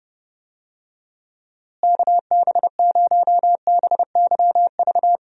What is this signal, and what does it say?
Morse code, heard as a keyed tone, about 31 wpm, 700 Hz. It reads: KB0BYV
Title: USA Call Signs - 6 Characters - Repeated 3x - 30wpm